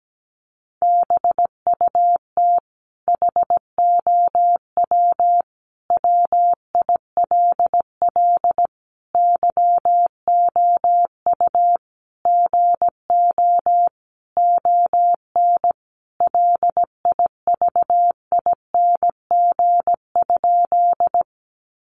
Morse code, heard as a keyed tone, about 17 wpm, 700 Hz. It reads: BUT HOW WILL YOU GO ON LIVING?
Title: Sentences from Top 1100 Words 17wpm